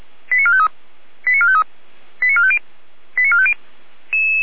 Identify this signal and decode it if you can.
Five-tone selective call (ZVEI), heard as quick runs of five tones, tones of about 70 ms. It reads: 89453 89453 89450 89450